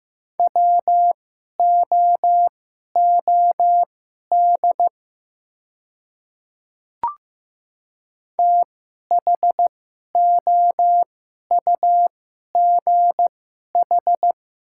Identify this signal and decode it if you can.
Morse code, heard as a keyed tone, about 15 wpm, spaced as for 11 wpm, 690 Hz. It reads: WOOD THOUGH
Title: 2x Character Spacing - Top 500 Words - 15wpm